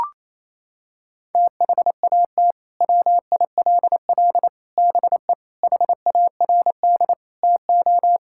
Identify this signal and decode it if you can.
Morse code, heard as a keyed tone, about 28 wpm, 700 Hz. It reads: THAT WILL BE HARD TO
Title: Sentences from Top 1000 Words - Review Part 5 - 28wpm